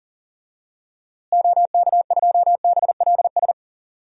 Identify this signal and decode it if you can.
Morse code, heard as a keyed tone, about 40 wpm, 700 Hz. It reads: OK2BLS
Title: DXCC-100 Call Signs - 6 Characters - Repeated 2x - 40wpm